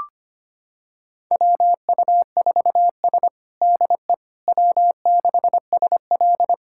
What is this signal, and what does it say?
Morse code, heard as a keyed tone, about 25 wpm, 700 Hz. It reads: WU4S DE W6SL
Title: USA Callsign DE Callsign - 4 Characters - 25wpm